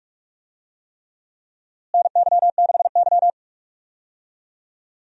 Morse code, heard as a keyed tone, about 46 wpm, 700 Hz. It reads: NY6Y
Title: USA Call Signs - 4 Characters - Repeated 3x - 45wpm